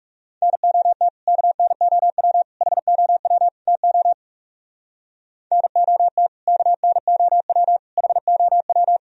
Morse code, heard as a keyed tone, about 34 wpm, 700 Hz. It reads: NOT KNOW HOW TO NOT KNOW HOW
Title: Speed-Racing - Sets of 4 Words 30wpm